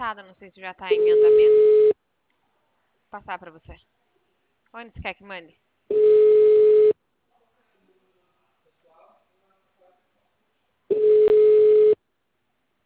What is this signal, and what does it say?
Ringback tone (Europe, 425 Hz), with speech.